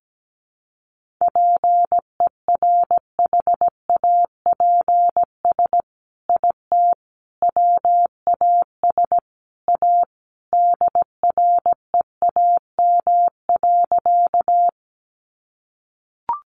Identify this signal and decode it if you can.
Morse code, heard as a keyed tone, about 17 wpm, 700 Hz. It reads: PERHAPS IT WAS A DREAM.
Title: Sentences from Top 1500 Words 17wpm